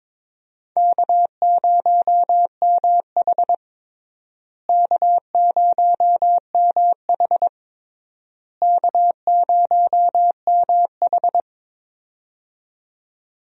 Morse code, heard as a keyed tone, about 22 wpm, 700 Hz. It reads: K0MH K0MH K0MH
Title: USA Call Signs - 4 Characters - Repeated 3x - 22wpm